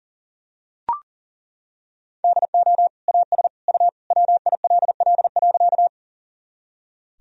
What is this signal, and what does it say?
Morse code, heard as a keyed tone, about 40 wpm, 700 Hz. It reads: DO AS U WILL.
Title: Sentences from Top 100 Words and Abbreviations 40wpm